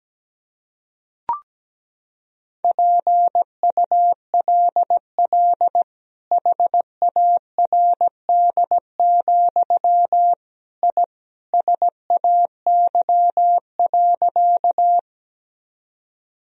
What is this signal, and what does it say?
Morse code, heard as a keyed tone, about 17 wpm, 700 Hz. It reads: PULL HARD, I SAY.
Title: Sentences from Top 1000 Words - Review Part 2 - 17wpm